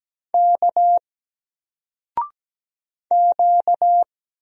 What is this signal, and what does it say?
Morse code, heard as a keyed tone, about 17 wpm, 700 Hz. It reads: K Q
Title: Single Character (Letter:Number:Punctuation:Prosign) Alt 17wpm